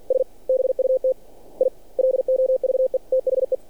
Morse code, heard as a keyed tone, about 48 wpm, 520 Hz. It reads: S BUT I LOVE THE
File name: archived / CRSnet